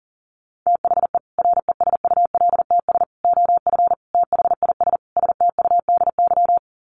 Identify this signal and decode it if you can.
Morse code, heard as a keyed tone, about 40 wpm, 700 Hz. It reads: THE RESULTS OF THIS STUDY
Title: Sets of 5 Words 40wpm